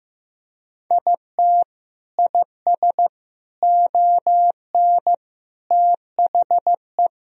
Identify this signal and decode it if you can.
Morse code, heard as a keyed tone, about 15 wpm, 700 Hz. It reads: IT IS ON THE
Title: Sentences from Top 1000 Words - Review Part 4 - 15wpm